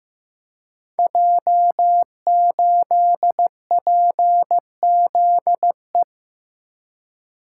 Morse code, heard as a keyed tone, about 15 wpm, 700 Hz. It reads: J8PZE
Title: Code Groups - 5 Letter:Numbers - 15wpm